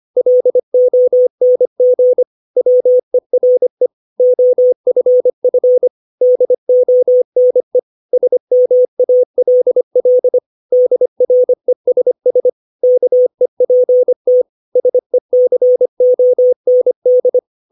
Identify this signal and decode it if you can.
Morse code, heard as a keyed tone, about 25 wpm, 500 Hz. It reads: LONG WERE OFF DONE SMALL DRESS KEPT SECOND
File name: 632 words / 632 Words 25WPM